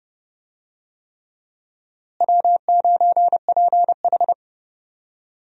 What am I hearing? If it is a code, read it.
Morse code, heard as a keyed tone, about 30 wpm, 700 Hz. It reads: W9PH
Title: USA Call Signs - 4 Characters - Repeated 3x - 30wpm